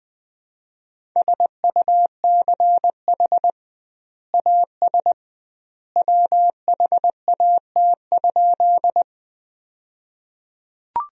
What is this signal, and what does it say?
Morse code, heard as a keyed tone, about 20 wpm, 700 Hz. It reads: SUCH AS WHAT?
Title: 2x Word-Spacing - Sentences from Top 200 Words - 20wpm